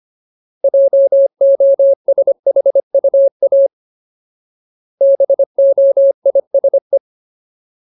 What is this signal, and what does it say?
Morse code, heard as a keyed tone, about 25 wpm, 550 Hz. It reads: JOSHUA BOISE
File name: Ultimate Word List / 25WPM Ultimate